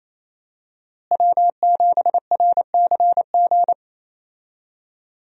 Morse code, heard as a keyed tone, about 28 wpm, 700 Hz. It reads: W7RCG
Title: USA Call Signs - 5 Characters - Repeated 2x - 28wpm